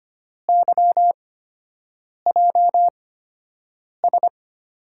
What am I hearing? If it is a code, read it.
Morse code, heard as a keyed tone, about 25 wpm, spaced as for 7 wpm, 700 Hz. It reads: YJS
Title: USA Call Signs - 6 Characters - 8x Character Spacing - 25wpm